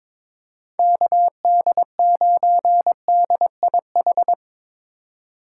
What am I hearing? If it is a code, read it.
Morse code, heard as a keyed tone, about 25 wpm, 700 Hz. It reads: KD9DIH
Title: Speed Racing - USA 2x3 Call Signs 22wpm